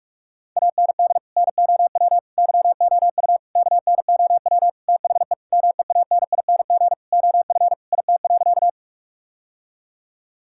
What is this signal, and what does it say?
Morse code, heard as a keyed tone, about 41 wpm, 700 Hz. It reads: AND NOW YOU KNOW THE MEANING OF IT.
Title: Sentences from Top 1000 Words - Review Part 6 - 45wpm